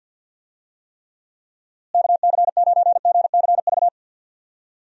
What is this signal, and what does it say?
Morse code, heard as a keyed tone, about 52 wpm, 700 Hz. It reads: KC9GCV
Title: USA Call Signs - 6 Characters - Repeated 2x - 50wpm